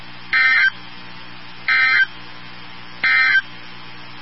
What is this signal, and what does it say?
Emergency Alert System (SAME) tones, heard as harsh buzzing bursts. It is heard from 0.2 to 3.5 s.
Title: NOAA Weather Radio 162.550